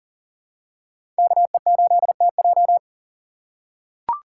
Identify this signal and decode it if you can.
Morse code, heard as a keyed tone, about 40 wpm, 700 Hz. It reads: KE8TJ
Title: USA Call Signs - 5 Characters - 3x Character Spacing - 40wpm